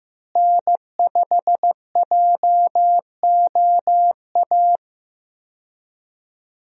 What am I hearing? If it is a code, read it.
Morse code, heard as a keyed tone, about 15 wpm, 700 Hz. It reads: N5JOA
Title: USA Call Signs - 5 Characters - Repeated 3x - 15wpm